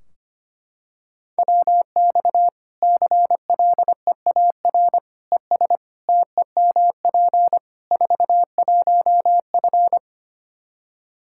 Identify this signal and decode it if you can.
Morse code, heard as a keyed tone, about 25 wpm, 700 Hz. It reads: WX CLEAR ES TEMP 41F
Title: K1USN SST QSO Practice 25wpm